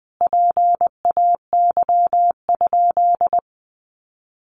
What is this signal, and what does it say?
Morse code, heard as a keyed tone, about 20 wpm, 700 Hz. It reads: PAY?
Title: Sentences from Top 700 Words 20wpm